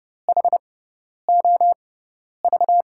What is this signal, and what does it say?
Morse code, heard as a keyed tone, about 30 wpm, spaced as for 10 wpm, 700 Hz. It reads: HOV
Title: USA Call Signs - 5 Characters - 6x Character Spacing - 30wpm